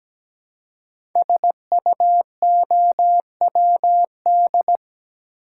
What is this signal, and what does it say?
Morse code, heard as a keyed tone, about 17 wpm, 700 Hz. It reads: SUOWD
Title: Code Groups - 5 Letters - 17wpm